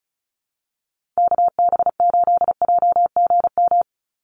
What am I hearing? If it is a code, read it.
Morse code, heard as a keyed tone, about 34 wpm, 700 Hz. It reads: KB8JGM
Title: USA Call Signs 35wpm